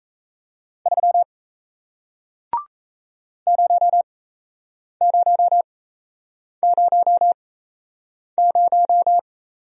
Morse code, heard as a keyed tone, about 38 wpm, spaced as for 8 wpm, 700 Hz. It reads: 3 0000